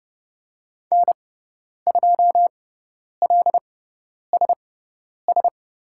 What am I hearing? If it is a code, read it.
Morse code, heard as a keyed tone, about 30 wpm, spaced as for 10 wpm, 700 Hz. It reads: N2LSS
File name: Lesson 17g - Letter u - Call Signs 30wpm